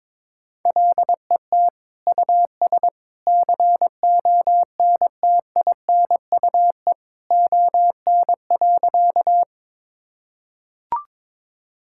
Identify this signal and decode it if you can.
Morse code, heard as a keyed tone, about 22 wpm, 700 Hz. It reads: LET US CONTINUE ON.
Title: Sentences from Top 1000 Words - Review Part 6 - 22wpm